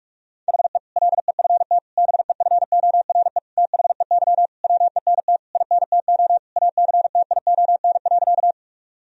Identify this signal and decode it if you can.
Morse code, heard as a keyed tone, about 46 wpm, 700 Hz. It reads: HE LEFT BEFORE THEY WENT INTO ACTION.
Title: Sentences from Top 1000 Words - Review Part 4 - 45wpm